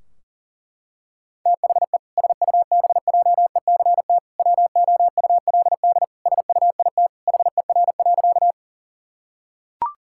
Morse code, heard as a keyed tone, about 37 wpm, 700 Hz. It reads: THE SUBJECT WOULD SUIT HER.
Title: Sentences from Top 1600 Words 40wpm